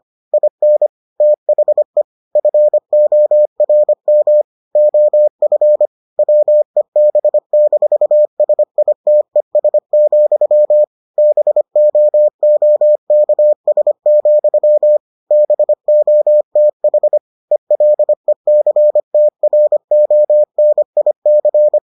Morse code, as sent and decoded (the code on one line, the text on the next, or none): .. -. / - .... / ..-. --- .-. -- / --- ..-. / .-- . -... -....- ... .. - . ... --..-- / -... --- --- -.- ... --..-- / -... --- - .... / . .-.. -.-. - .-. --- -. .. -.-.
IN THE FORM OF WEB-SITES, BOOKS, BOTH ELECTRONIC